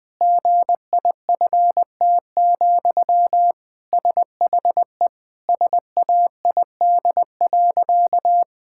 Morse code, heard as a keyed tone, about 20 wpm, 700 Hz. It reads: GIFT, SHE SAID.
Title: Sentences from Top 1000 Words - Review Part 6 - 20wpm